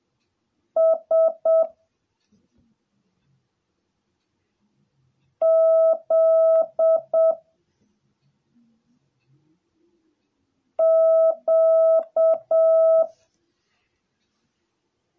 Morse code, heard as a keyed tone, about 7 wpm, 650 Hz. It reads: S Z Q